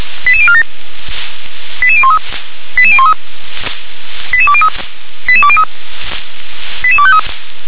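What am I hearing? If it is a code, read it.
Five-tone selective call (ZVEI), heard as quick runs of five tones, tones of about 70 ms. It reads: 88047 88013 88013 88203 88203 88253
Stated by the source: ZVEI five-tone selective call received on an SDR.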